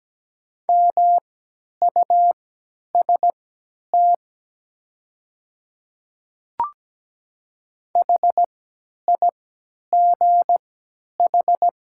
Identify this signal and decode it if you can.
Morse code, heard as a keyed tone, about 17 wpm, spaced as for 10 wpm, 700 Hz. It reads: MUST HIGH
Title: 3x Character Spacing - Top 200 Words - 17wpm